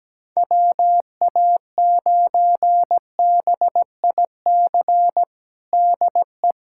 Morse code, heard as a keyed tone, about 17 wpm, 700 Hz. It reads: WA9BIC DE
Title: USA Callsign DE Callsign 17wpm